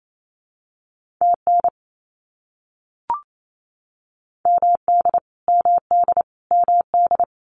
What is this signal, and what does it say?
Morse code, heard as a keyed tone, about 28 wpm, 700 Hz. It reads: TN MD MD MD
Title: US States Repeated 3x 28wpm